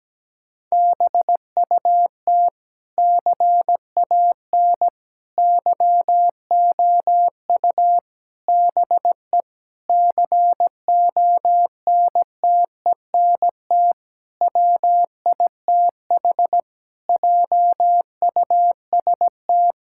Morse code, heard as a keyed tone, about 17 wpm, 700 Hz. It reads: BUT CAN YOU BE CONTENT WITH JUST
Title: Sentences from Top 1000 Words - Review Part 6 - 17wpm